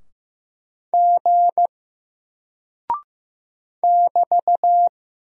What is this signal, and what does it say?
Morse code, heard as a keyed tone, about 15 wpm, 700 Hz. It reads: G <BT>
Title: Single Character (Letter:Number:Punctuation:Prosign) 15wpm